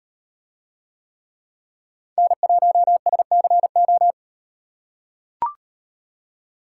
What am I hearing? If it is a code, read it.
Morse code, heard as a keyed tone, about 38 wpm, 700 Hz. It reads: N1SCO